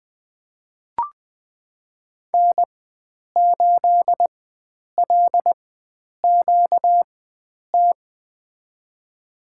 Morse code, heard as a keyed tone, about 20 wpm, spaced as for 9 wpm, 700 Hz. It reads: N8LQT
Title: USA Call Signs - 5 Characters - 4x Character Spacing - 20wpm